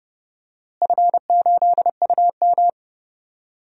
Morse code, heard as a keyed tone, about 30 wpm, 700 Hz. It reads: F8UM
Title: DXCC-100 Call Signs - 4 Characters - Repeated 2x - 30wpm